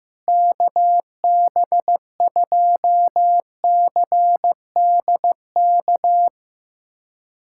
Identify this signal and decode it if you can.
Morse code, heard as a keyed tone, about 15 wpm, 700 Hz. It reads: KB2CDK
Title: USA Callsign DE Callsign 15wpm